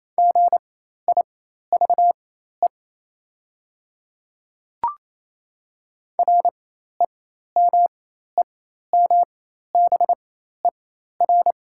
Morse code, heard as a keyed tone, about 28 wpm, spaced as for 13 wpm, 700 Hz. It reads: GIVE REMEMBER
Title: 4x Character Spacing - Top 500 Words - 28wpm